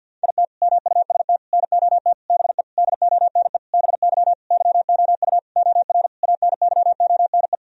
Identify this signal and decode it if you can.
Morse code, heard as a keyed tone, about 52 wpm, 700 Hz. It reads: IT MUST NOT BE DONE BY YOU OR ANYONE
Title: Sentences from Top 1000 Words - Review Part 1 - 50wpm